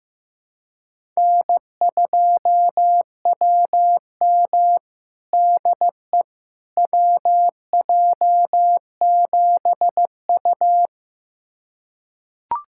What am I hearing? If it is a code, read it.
Morse code, heard as a keyed tone, about 16 wpm, 700 Hz. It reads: N2WM DE WJ7U
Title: USA Callsign DE Callsign - 4 Characters - 15wpm